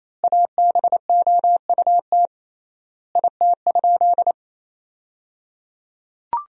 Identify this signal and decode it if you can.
Morse code, heard as a keyed tone, about 28 wpm, 700 Hz. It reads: ABOUT IT?
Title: 3x Word-Spacing - Sentences from Top 100 Words - 28wpm